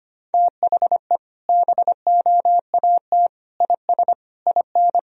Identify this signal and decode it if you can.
Morse code, heard as a keyed tone, about 25 wpm, 700 Hz. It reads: THE BOAT IS IN
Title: Sentences from Top 1000 Words - Review Part 1 - 25wpm